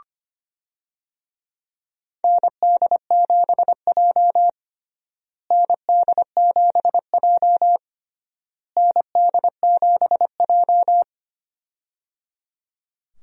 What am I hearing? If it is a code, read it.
Morse code, heard as a keyed tone, about 25 wpm, 700 Hz. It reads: ND7J ND7J ND7J
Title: USA Call Signs - 4 Characters - Repeated 3x - 25wpm